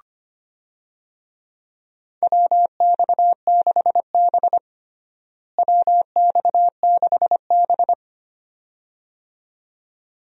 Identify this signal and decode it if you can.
Morse code, heard as a keyed tone, about 25 wpm, 700 Hz. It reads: WX6B WX6B
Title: USA Call Signs - 4 Characters - Repeated 2x - 25wpm